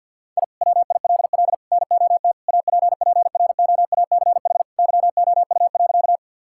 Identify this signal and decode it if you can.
Morse code, heard as a keyed tone, about 50 wpm, 700 Hz. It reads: I WILL NOT APPROACH YOU.
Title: Sentences from Top 1000 Words - Review Part 6 - 50wpm